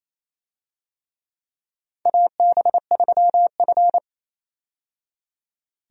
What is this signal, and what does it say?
Morse code, heard as a keyed tone, about 28 wpm, 690 Hz. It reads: AB3F
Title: USA Call Signs - 4 Characters - 28wpm